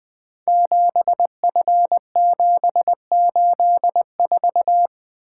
Morse code, heard as a keyed tone, about 20 wpm, 700 Hz. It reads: 7F784
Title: Code Groups - 5 Letter:Numbers - 20wpm